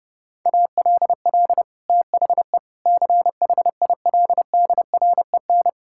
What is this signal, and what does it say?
Morse code, heard as a keyed tone, about 30 wpm, 700 Hz. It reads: ALL THE CHILDREN
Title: Sentences from Top 1000 Words - Review Part 6 - 30wpm